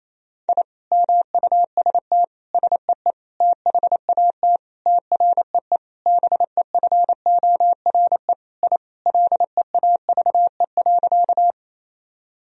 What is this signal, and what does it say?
Morse code, heard as a keyed tone, about 27 wpm, 700 Hz. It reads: I MUST SEE THAT TREE BEFORE I LEAVE.
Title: Sentences from Top 1000 Words - Review Part 1 - 28wpm